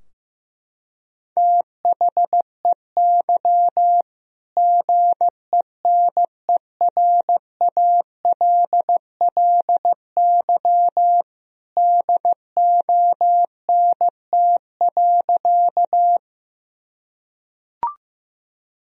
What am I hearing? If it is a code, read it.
Morse code, heard as a keyed tone, about 15 wpm, 700 Hz. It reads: THEY GENERALLY DONT.
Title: Sentences from Top 1000 Words - Review Part 4 - 15wpm